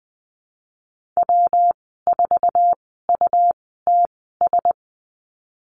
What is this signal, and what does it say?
Morse code, heard as a keyed tone, about 20 wpm, spaced as for 15 wpm, 700 Hz. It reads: W4UTS
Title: 154 Lesson 25g - Letter p - Call Signs 20wpm